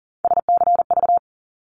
Morse code, heard as a keyed tone, about 40 wpm, 700 Hz. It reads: SCV
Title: ARRL Sections 40wpm